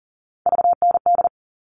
Morse code, heard as a keyed tone, about 40 wpm, 700 Hz. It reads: VND
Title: All Cut Numbers - 3-Digit 40wpm